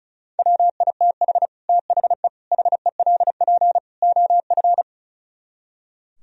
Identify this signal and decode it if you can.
Morse code, heard as a keyed tone, about 45 wpm, 700 Hz. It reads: WITH THE HELP OF